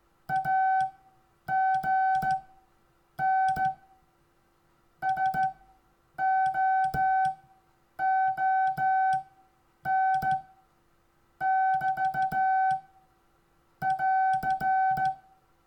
Morse code, heard as a keyed tone, about 15 wpm, spaced as for 8 wpm, 760 Hz. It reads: AGN SOON <BT> <AR>
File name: cw / Lesson26 SUMMARY